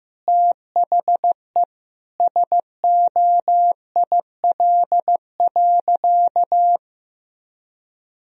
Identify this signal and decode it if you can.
Morse code, heard as a keyed tone, about 16 wpm, 700 Hz. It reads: THE SOIL.
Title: Sentences from Top 1000 Words - Review Part 4 - 15wpm